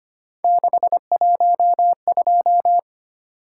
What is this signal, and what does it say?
Morse code, heard as a keyed tone, about 25 wpm, 700 Hz. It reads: 612
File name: Three Letter-Number Combinations - 25wpm